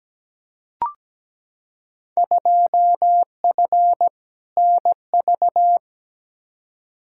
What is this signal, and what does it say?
Morse code, heard as a keyed tone, about 17 wpm, 700 Hz. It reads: 2F NV
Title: Field Day Exchanges - 17wpm